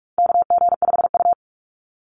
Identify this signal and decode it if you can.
Morse code, heard as a keyed tone, about 44 wpm, 700 Hz. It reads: KG5U
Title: Call Signs Repeated 3x - 45wpm